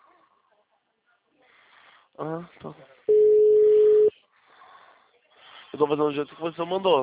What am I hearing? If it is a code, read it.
Ringback tone (Europe, 425 Hz), with speech.